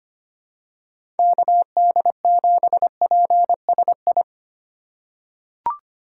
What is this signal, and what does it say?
Morse code, heard as a keyed tone, about 25 wpm, 700 Hz. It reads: KD7PSI
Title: USA Call Signs - 4x Character Spacing - 25wpm